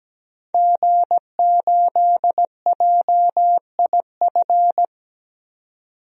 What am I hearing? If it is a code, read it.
Morse code, heard as a keyed tone, about 17 wpm, 700 Hz. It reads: G8JIF
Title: DXCC-100 Call Signs - 5 Characters - 17wpm